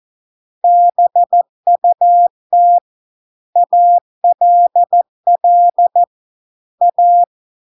Morse code, heard as a keyed tone, about 14 wpm, 700 Hz. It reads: BUT ALL A